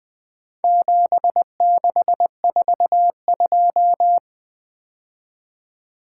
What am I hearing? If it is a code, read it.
Morse code, heard as a keyed tone, about 22 wpm, 700 Hz. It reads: 7642
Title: Speed-Racing - Four Letter-Number Combinations - 20wpm